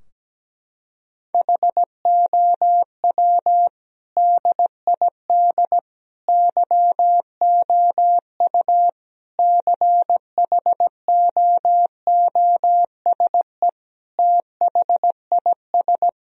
Morse code, heard as a keyed tone, about 17 wpm, 700 Hz. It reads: HOW DID YOU CHOOSE THIS
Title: Sentences from Top 1000 Words - Review Part 4 - 17wpm